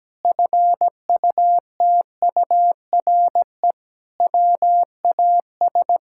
Morse code, heard as a keyed tone, about 17 wpm, 700 Hz. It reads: FUTURE WAS
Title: Sentences from Top 1000 Words - Review Part 6 - 17wpm